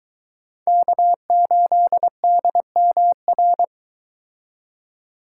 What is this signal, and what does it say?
Morse code, heard as a keyed tone, about 22 wpm, 700 Hz. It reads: K8DMR